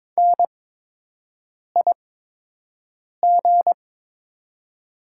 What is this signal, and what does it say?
Morse code, heard as a keyed tone, about 22 wpm, spaced as for 6 wpm, 700 Hz. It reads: NIG
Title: 8x Character Spacing - Top 500 Words - 22wpm